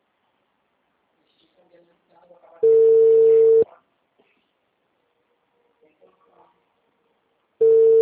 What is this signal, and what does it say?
Ringback tone (China, 450 Hz), with speech.